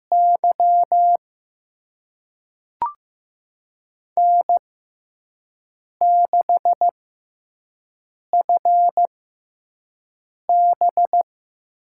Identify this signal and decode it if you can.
Morse code, heard as a keyed tone, about 15 wpm, spaced as for 5 wpm, 700 Hz. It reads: Y N6FB
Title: USA Call Signs - 5 Characters - 6x Character Spacing - 15wpm